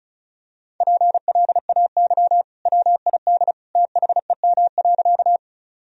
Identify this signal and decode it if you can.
Morse code, heard as a keyed tone, about 36 wpm, 700 Hz. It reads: PLAY WID THEM.
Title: Sentences from Top 200 Words and Abbreviations 35wpm